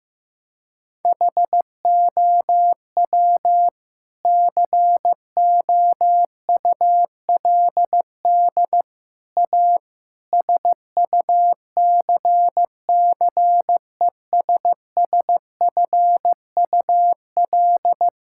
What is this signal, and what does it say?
Morse code, heard as a keyed tone, about 15 wpm, 700 Hz. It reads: HOW COULD A SUCCESSFUL
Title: Sentences from Top 1000 Words - Review Part 6 - 15wpm